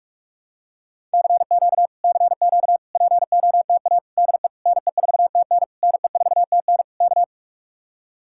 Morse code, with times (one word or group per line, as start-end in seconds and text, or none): 1.1-1.9 CQ
2.0-2.8 CQ
3.0-4.0 POTA
4.2-4.5 DE
4.7-5.6 NE4TN
5.8-6.8 NE4TN
7.0-7.2 K